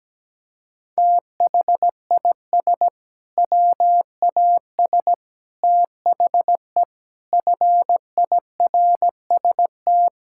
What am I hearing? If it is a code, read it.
Morse code, heard as a keyed tone, about 17 wpm, 700 Hz. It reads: THIS WAS THE FIRST
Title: Sentences from Top 1600 Words 17wpm